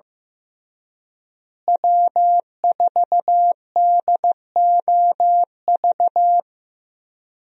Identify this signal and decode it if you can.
Morse code, heard as a keyed tone, about 15 wpm, 700 Hz. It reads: W4DOV
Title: USA Call Signs - 5 Characters - Repeated 3x - 15wpm